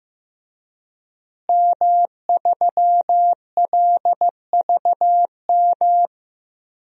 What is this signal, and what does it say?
Morse code, heard as a keyed tone, about 15 wpm, 700 Hz. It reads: M3LVM
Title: DXCC-100 Call Signs - 5 Characters - Repeated 2x - 15wpm